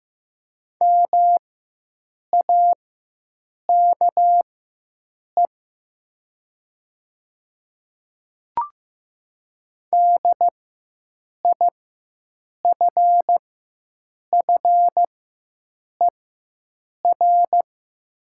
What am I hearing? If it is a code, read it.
Morse code, heard as a keyed tone, about 15 wpm, spaced as for 7 wpm, 700 Hz. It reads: MAKE DIFFER
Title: 4x Character Spacing - Top 300 Words - 15wpm